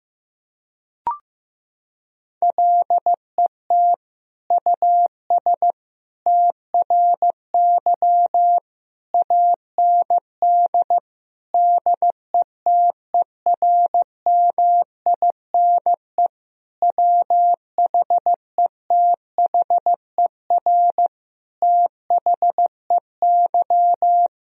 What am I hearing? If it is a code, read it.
Morse code, heard as a keyed tone, about 16 wpm, 700 Hz. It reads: LET US TRY AND DETERMINE WHETHER THEY
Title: Sentences from Top 1000 Words - Review Part 4 - 15wpm